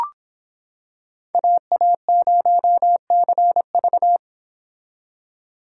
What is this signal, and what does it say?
Morse code, heard as a keyed tone, about 25 wpm, 700 Hz. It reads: AA0CV